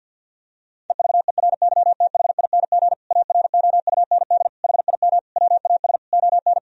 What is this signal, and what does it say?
Morse code, heard as a keyed tone, about 50 wpm, 700 Hz. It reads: EVERYTHING AROUND HIM WAS ON
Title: Sentences from Top 1000 Words - Review Part 3 - 50wpm